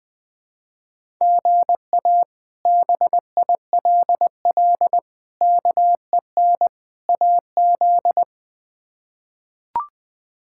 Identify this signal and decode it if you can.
Morse code, heard as a keyed tone, about 20 wpm, 700 Hz. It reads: GA BILL KEN AZ
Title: K1USN SST QSO Practice - 20wpm